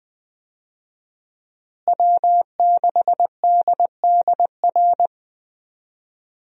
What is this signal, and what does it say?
Morse code, heard as a keyed tone, about 20 wpm, 700 Hz. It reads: W6DDR
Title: USA Call Signs - 5 Characters - Repeated 3x - 20wpm